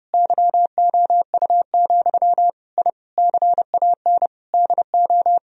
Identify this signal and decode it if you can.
Morse code, heard as a keyed tone, about 32 wpm, 700 Hz. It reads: YOU, I CAN DO
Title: Sentences from Top 1000 Words - Review Part 3 - 30wpm